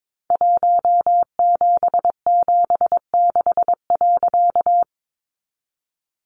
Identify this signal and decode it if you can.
Morse code, heard as a keyed tone, about 22 wpm, 700 Hz. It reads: 1776.
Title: A Sound Of Thunder 22wpm